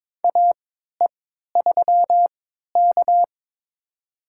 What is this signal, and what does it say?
Morse code, heard as a keyed tone, about 21 wpm, spaced as for 12 wpm, 700 Hz. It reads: AE3K
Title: USA Call Signs - 4 Characters - 3x Character Spacing - 22wpm